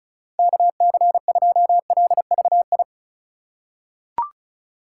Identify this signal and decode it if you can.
Morse code, heard as a keyed tone, about 34 wpm, 700 Hz. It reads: KC2LVI
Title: USA Call Signs - 6 Characters - 3x Character Spacing - 35wpm